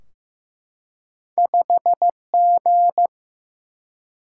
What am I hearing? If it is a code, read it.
Morse code, heard as a keyed tone, about 15 wpm, 700 Hz. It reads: 5G
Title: Single-Letter Single-Number 15wpm